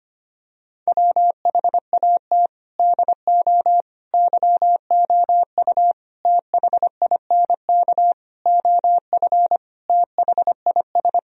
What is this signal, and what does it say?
Morse code, heard as a keyed tone, about 25 wpm, 700 Hz. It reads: WHAT DO YOU THINK OF THIS
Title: Sentences from Top 1000 Words - Review Part 2 - 25wpm